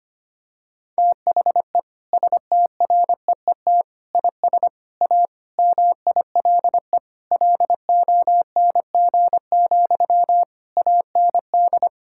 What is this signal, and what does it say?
Morse code, heard as a keyed tone, about 25 wpm, 700 Hz. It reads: THE STREET IS A MILE LONG, AND
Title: Sentences from Top 1600 Words 25wpm